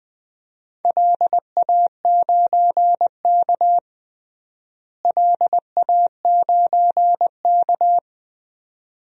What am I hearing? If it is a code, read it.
Morse code, heard as a keyed tone, about 20 wpm, 700 Hz. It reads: LA9K LA9K
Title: DXCC-100 Call Signs - 4 Characters - Repeated 2x - 20wpm